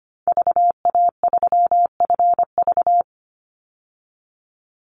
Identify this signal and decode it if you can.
Morse code, heard as a keyed tone, about 25 wpm, 700 Hz. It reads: VA3FV
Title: Call Signs - 25wpm